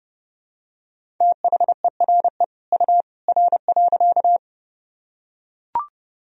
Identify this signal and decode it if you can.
Morse code, heard as a keyed tone, about 30 wpm, 700 Hz. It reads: THERE U R.
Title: Sentences from Top 100 Words and Abbreviations 30wpm